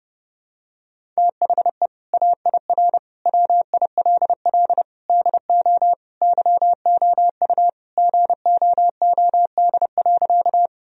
Morse code, heard as a keyed tone, about 30 wpm, 700 Hz. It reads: THE AIR WILL DO YOU GOOD.
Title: Sentences from Top 1000 Words - Review Part 1 - 30wpm